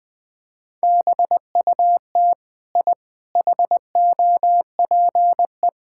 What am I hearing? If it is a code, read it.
Morse code, heard as a keyed tone, about 20 wpm, 700 Hz. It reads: BUT I HOPE
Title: Sentences from Top 1000 Words - Review Part 6 - 20wpm